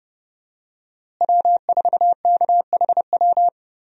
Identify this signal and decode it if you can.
Morse code, heard as a keyed tone, about 30 wpm, 700 Hz. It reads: W4KHW
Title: USA Call Signs - 5 Characters - Repeated 2x - 30wpm